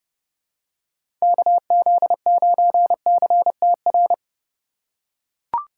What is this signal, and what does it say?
Morse code, heard as a keyed tone, about 30 wpm, 700 Hz. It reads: KZ9CTR
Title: USA Call Signs - 6 Characters - 4x Character Spacing - 30wpm